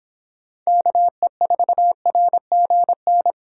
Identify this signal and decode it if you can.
Morse code, heard as a keyed tone, about 26 wpm, 700 Hz. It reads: KE4RGN